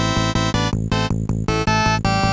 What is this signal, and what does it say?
There is no code.